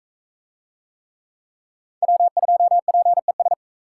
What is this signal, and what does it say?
Morse code, heard as a keyed tone, about 39 wpm, 700 Hz. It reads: W2PES